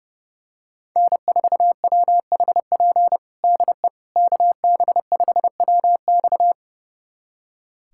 Morse code, heard as a keyed tone, about 32 wpm, 700 Hz. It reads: N4WHP DE KB5WX
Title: USA Callsign DE Callsign - 5 Characters - 30wpm